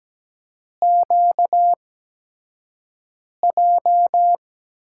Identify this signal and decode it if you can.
Morse code, heard as a keyed tone, about 17 wpm, spaced as for 5 wpm, 700 Hz. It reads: QJ
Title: USA Call Signs - 5 Characters - 8x Character Spacing - 17wpm